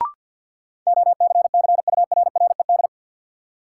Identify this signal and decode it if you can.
Morse code, heard as a keyed tone, about 53 wpm, 700 Hz. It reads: OCCURRED
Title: Sets of 5 Words Encore - New Word Drill - 50wpm